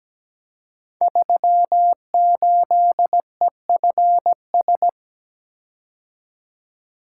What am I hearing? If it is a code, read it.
Morse code, heard as a keyed tone, about 17 wpm, 700 Hz. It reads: 38EFS
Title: Code Groups - 5 Letter:Numbers - 17wpm